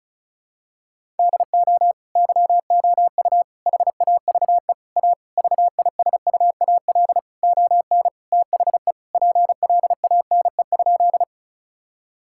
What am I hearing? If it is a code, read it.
Morse code, heard as a keyed tone, about 34 wpm, 700 Hz. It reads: DO YOU HAVE A VISUAL ON THE PLANE?
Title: Sentences from Top 1000 Words - Review Part 6 - 35wpm